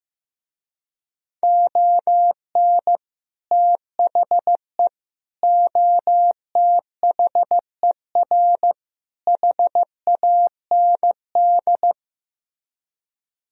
Morse code, heard as a keyed tone, about 15 wpm, 700 Hz. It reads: ON THE OTHER HAND
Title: Speed-Racing - Sets of 4 Words 15wpm